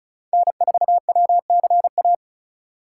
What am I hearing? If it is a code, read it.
Morse code, heard as a keyed tone, about 36 wpm, 700 Hz. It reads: N4WCA
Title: USA Call Signs - 3x Character Spacing - 35wpm